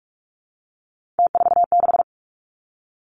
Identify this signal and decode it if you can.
Morse code, heard as a keyed tone, about 46 wpm, 700 Hz. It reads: T46
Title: Three Challenging Letter-Number Combinations 45wpm